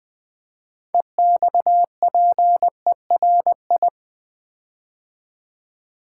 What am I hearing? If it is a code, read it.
Morse code, heard as a keyed tone, about 20 wpm, 700 Hz. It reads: EXPERI
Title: Build A Word - 6 Letters At A Time - 12-Letter Words - 20wpm